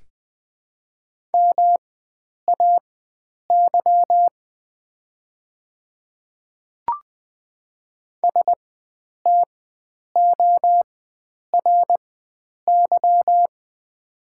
Morse code, heard as a keyed tone, about 20 wpm, spaced as for 9 wpm, 700 Hz. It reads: MAY STORY
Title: 4x Character Spacing - Top 300 Words - 20wpm